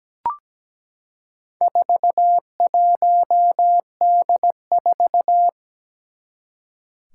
Morse code, heard as a keyed tone, about 17 wpm, 700 Hz. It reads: 41D4
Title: Four Challenging Letter-Number Combinations 17wpm